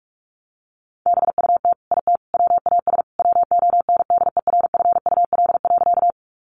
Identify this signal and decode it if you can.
Morse code, heard as a keyed tone, about 44 wpm, 700 Hz. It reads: BUT IT WAS WONDERFUL.
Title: Sentences from Top 1100 Words 45wpm